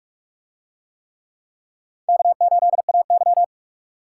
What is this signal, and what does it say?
Morse code, heard as a keyed tone, about 38 wpm, 700 Hz. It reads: K8AY